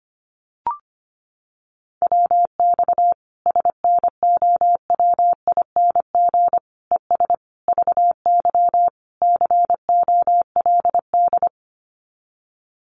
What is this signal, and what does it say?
Morse code, heard as a keyed tone, about 25 wpm, 700 Hz. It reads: WX SNOWING ES VY COLD
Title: Common QSO Phrases 25wpm